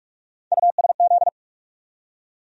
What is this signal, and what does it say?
Morse code, heard as a keyed tone, about 47 wpm, 700 Hz. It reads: USZ
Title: USA Call Sign Postfixes - 3 Letters 45wpm